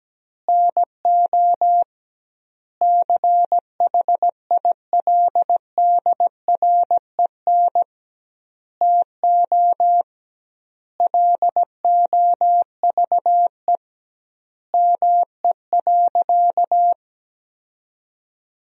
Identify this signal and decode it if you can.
Morse code, heard as a keyed tone, about 17 wpm, 700 Hz. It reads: NO CHILDREN TO LOVE ME.
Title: 2x Word-Spacing - Sentences from Top 400 Words - 17wpm